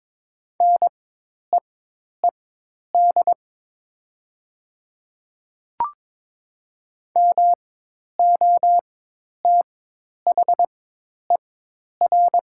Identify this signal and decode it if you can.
Morse code, heard as a keyed tone, about 22 wpm, spaced as for 10 wpm, 700 Hz. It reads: NEED MOTHER
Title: 4x Character Spacing - Top 200 Words - 22wpm